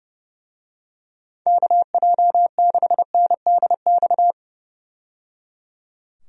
Morse code, heard as a keyed tone, about 30 wpm, 700 Hz. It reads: KJ6NDX
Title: USA Call Signs - 6 Characters - 30wpm